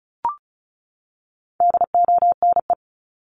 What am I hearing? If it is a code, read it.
Morse code, heard as a keyed tone, about 34 wpm, 700 Hz. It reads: DONE
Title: Top 500 Words 35wpm